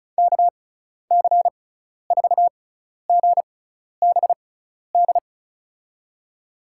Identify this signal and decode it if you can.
Morse code, heard as a keyed tone, about 34 wpm, spaced as for 12 wpm, 700 Hz. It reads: KC4GBD
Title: USA Call Signs - 6 Characters - 6x Character Spacing - 35wpm